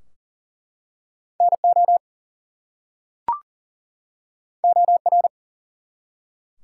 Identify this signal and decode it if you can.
Morse code, heard as a keyed tone, about 42 wpm, 700 Hz. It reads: NO OR
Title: Two Letter Words - 40wpm